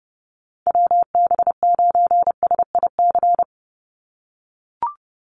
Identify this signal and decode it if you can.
Morse code, heard as a keyed tone, about 30 wpm, 700 Hz. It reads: WB9SIC
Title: USA Call Signs 30wpm